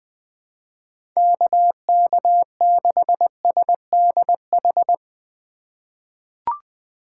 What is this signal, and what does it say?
Morse code, heard as a keyed tone, about 20 wpm, 700 Hz. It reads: KK6SDH
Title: USA Call Signs - 6 Characters - 3x Character Spacing - 20wpm